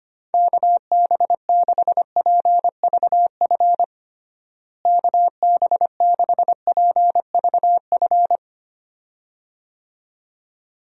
Morse code, heard as a keyed tone, about 25 wpm, 700 Hz. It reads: KB6PVF KB6PVF
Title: USA Call Signs - 6 Characters - Repeated 2x - 25wpm